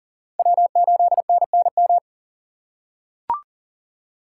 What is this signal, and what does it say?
Morse code, heard as a keyed tone, about 40 wpm, 700 Hz. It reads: W8NNM
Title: USA Call Signs - 5 Characters - 3x Character Spacing - 40wpm